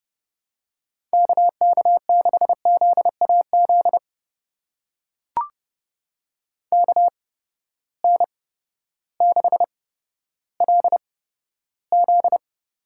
Morse code, heard as a keyed tone, about 30 wpm, spaced as for 14 wpm, 700 Hz. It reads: KK6ZAZ KN6LZ
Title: USA Call Signs - 8x Character Spacing - 30wpm